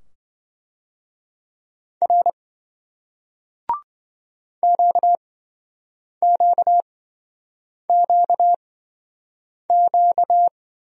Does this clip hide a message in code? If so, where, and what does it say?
Yes: Morse code from 2.0 to 10.5 s, heard as a keyed tone, about 28 wpm, spaced as for 7 wpm, 700 Hz. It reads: R QQQQ